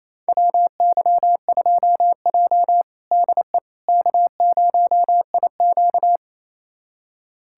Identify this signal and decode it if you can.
Morse code, heard as a keyed tone, about 28 wpm, 700 Hz. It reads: WY2J DE K0IQ
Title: USA Callsign DE Callsign - 4 Characters - 28wpm